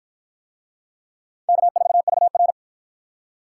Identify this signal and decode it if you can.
Morse code, heard as a keyed tone, about 50 wpm, 700 Hz. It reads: K4VR